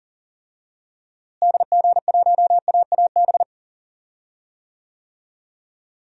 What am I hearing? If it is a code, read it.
Morse code, heard as a keyed tone, about 40 wpm, 700 Hz. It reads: DG1AAB
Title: DXCC-100 Call Signs - 6 Characters - Repeated 2x - 40wpm